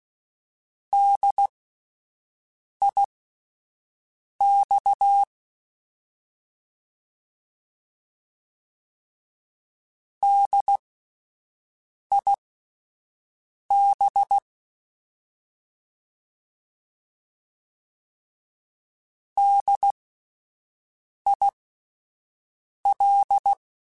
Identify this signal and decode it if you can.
Morse code, heard as a keyed tone, about 16 wpm, spaced as for 5 wpm, 800 Hz. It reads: DIX DIB DIL